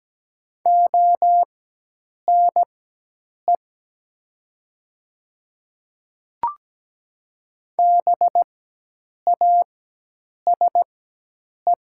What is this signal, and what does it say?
Morse code, heard as a keyed tone, about 17 wpm, spaced as for 8 wpm, 700 Hz. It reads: ONE BASE
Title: 4x Character Spacing - Top 500 Words - 17wpm